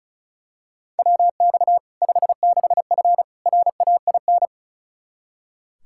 Morse code, heard as a keyed tone, about 36 wpm, 700 Hz. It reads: WX 56F RAIN
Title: Practice QSO - Exchange 2 - Info - 35wpm